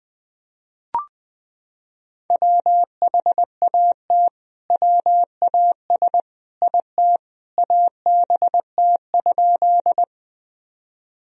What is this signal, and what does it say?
Morse code, heard as a keyed tone, about 21 wpm, 700 Hz. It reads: WHAT WAS IT ABT?
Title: Sentences from Top 100 Words and Abbreviations 20wpm